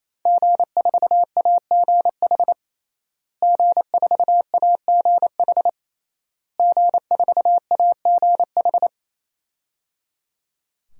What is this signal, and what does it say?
Morse code, heard as a keyed tone, about 28 wpm, 700 Hz. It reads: G4AGH G4AGH G4AGH
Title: DXCC-100 Call Signs - 5 Characters - Repeated 3x - 28wpm